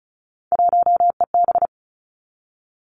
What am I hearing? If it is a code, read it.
Morse code, heard as a keyed tone, about 36 wpm, 700 Hz. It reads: 1EB
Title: Three Challenging Letter-Number Combinations 35wpm